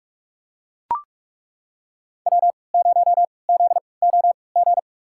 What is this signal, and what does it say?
Morse code, heard as a keyed tone, about 41 wpm, spaced as for 25 wpm, 700 Hz. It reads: W0ZOG
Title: USA Call Signs - 5 Characters - 3x Character Spacing - 45wpm